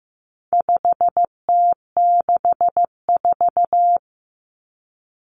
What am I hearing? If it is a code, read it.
Morse code, heard as a keyed tone, about 15 wpm, 700 Hz. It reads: 5T64
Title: Common Cut Numbers - 4-Digit 15wpm